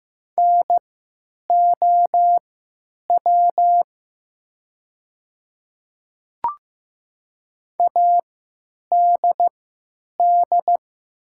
Morse code, heard as a keyed tone, about 15 wpm, spaced as for 9 wpm, 700 Hz. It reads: NOW ADD
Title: 3x Character Spacing - Top 200 Words - 15wpm